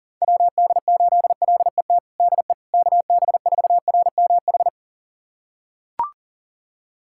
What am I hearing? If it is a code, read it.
Morse code, heard as a keyed tone, about 40 wpm, 700 Hz. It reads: WD8LET DE KB4RMH
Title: USA Callsign DE Callsign 40wpm